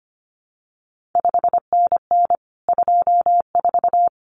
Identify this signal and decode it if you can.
Morse code, heard as a keyed tone, about 25 wpm, 700 Hz. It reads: 5NN 24
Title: CQ WW DX Contest Exchange Encore - 25wpm